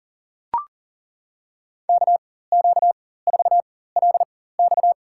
Morse code, heard as a keyed tone, about 40 wpm, spaced as for 19 wpm, 700 Hz. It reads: KQ4LX
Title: USA Call Signs - 4x Character Spacing - 40wpm